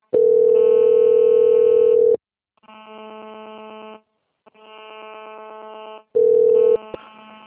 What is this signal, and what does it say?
Ringback tone (North America), 440 + 480 Hz.
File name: out- - - - - .16940